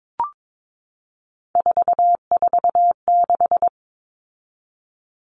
Morse code, heard as a keyed tone, about 22 wpm, 700 Hz. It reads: 446
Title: Numbers - 3-Digit 22wpm